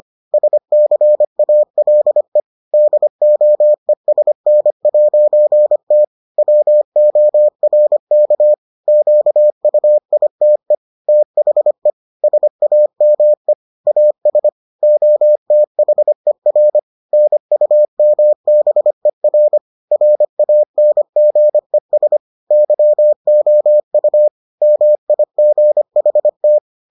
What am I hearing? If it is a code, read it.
Morse code, heard as a keyed tone, about 25 wpm, 600 Hz. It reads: SCALE DOESN'T WORK QUITE THE SAME AS OTHER NUMBER RANGES YOU MIGHT